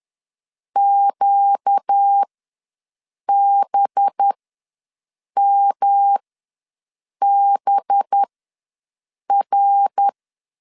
Morse code, heard as a keyed tone, about 10 wpm, spaced as for 6 wpm, 800 Hz. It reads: QBMBR
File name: LEZIONI / 5 LBVC / 5 LB ws 10wpm 3616